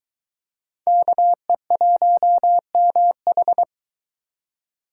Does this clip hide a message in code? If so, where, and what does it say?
Yes: Morse code from 0.9 to 3.6 s, heard as a keyed tone, about 23 wpm, 700 Hz. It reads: KE1MH